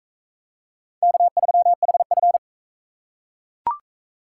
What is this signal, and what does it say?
Morse code, heard as a keyed tone, about 41 wpm, 700 Hz. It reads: K3HF